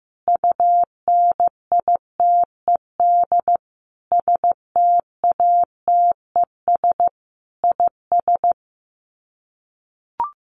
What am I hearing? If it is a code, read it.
Morse code, heard as a keyed tone, about 15 wpm, 700 Hz. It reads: UNITED STATES IS
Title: Sets of 5 Words Encore2 15wpm